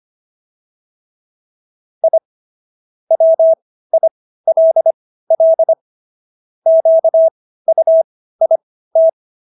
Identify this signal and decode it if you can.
Morse code, heard as a keyed tone, about 25 wpm, spaced as for 15 wpm, 650 Hz. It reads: I WILL QUIT